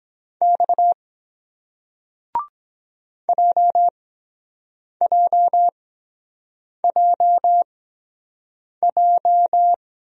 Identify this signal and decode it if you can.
Morse code, heard as a keyed tone, about 23 wpm, spaced as for 7 wpm, 700 Hz. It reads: X JJJJ